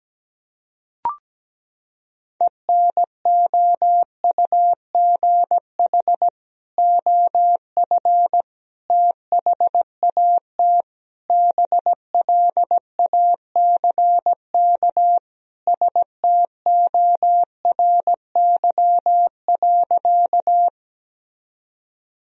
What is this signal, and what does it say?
Morse code, heard as a keyed tone, about 17 wpm, 700 Hz. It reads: ENOUGH OF THAT BLACK STORY.
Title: Sentences from Top 1000 Words - Review Part 2 - 17wpm